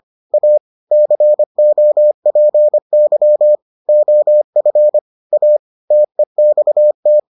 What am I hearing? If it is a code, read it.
Morse code, heard as a keyed tone, about 25 wpm, 600 Hz. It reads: A COPY OF A TEXT